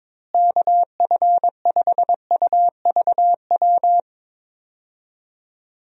Morse code, heard as a keyed tone, about 22 wpm, 700 Hz. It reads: KF5UVW
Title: USA Call Signs - 6 Characters - Repeated 2x - 22wpm